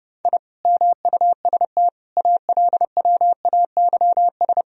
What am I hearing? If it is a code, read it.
Morse code, heard as a keyed tone, about 30 wpm, 700 Hz. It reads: I MUST ALWAYS
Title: Sentences from Top 1000 Words - Review Part 3 - 30wpm